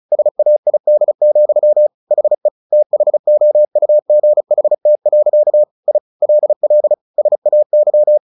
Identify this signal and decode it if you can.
Morse code, heard as a keyed tone, about 34 wpm, 600 Hz. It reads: SAID, HE THOUGHT. I LL SAY
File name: the old man and the sea 2 35wpm